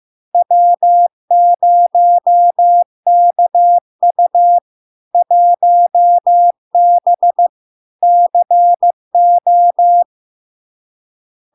Morse code, heard as a keyed tone, about 15 wpm, 700 Hz. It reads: W0KU 1B CO
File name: LARK-FD-2022-Log-in-CW-15wpm-700-Hz